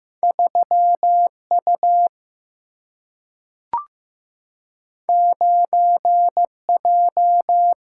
Morse code, heard as a keyed tone, about 15 wpm, 700 Hz. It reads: 3 U 9 J
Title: Single-Letter Single-Number 15wpm